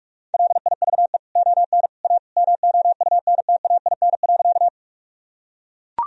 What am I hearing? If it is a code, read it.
Morse code, heard as a keyed tone, about 41 wpm, 700 Hz. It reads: LIVE ON A MOUNTAIN.
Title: Sentences from Top 1000 Words - Review Part 1 - 45wpm